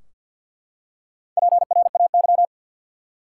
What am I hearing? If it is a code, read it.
Morse code, heard as a keyed tone, about 52 wpm, 700 Hz. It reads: PRAY
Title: Four Letter Words - 50wpm